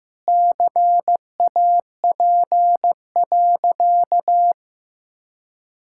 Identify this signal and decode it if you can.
Morse code, heard as a keyed tone, about 15 wpm, 700 Hz. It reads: CAP.
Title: Sentences from Top 1600 Words 15wpm